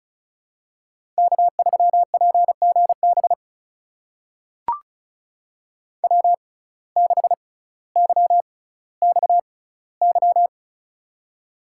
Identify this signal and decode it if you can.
Morse code, heard as a keyed tone, about 36 wpm, spaced as for 18 wpm, 700 Hz. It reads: K3PGB W6YXY
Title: USA Call Signs - 5 Characters - 6x Character Spacing - 35wpm